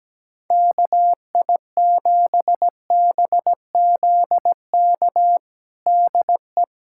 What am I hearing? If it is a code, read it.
Morse code, heard as a keyed tone, about 17 wpm, 700 Hz. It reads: KI7BZK DE
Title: USA Callsign DE Callsign - 6 Characters - 17wpm